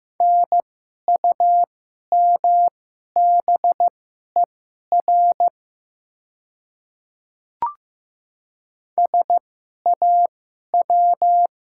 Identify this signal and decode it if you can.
Morse code, heard as a keyed tone, about 15 wpm, spaced as for 11 wpm, 700 Hz. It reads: NUMBER SAW
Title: 2x Character Spacing - Top 300 Words - 15wpm